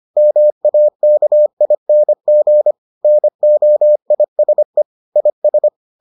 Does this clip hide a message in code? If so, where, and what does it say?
Yes: Morse code from 0.2 to 5.7 s, heard as a keyed tone, about 25 wpm, 600 Hz. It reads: MAKING NOISE IS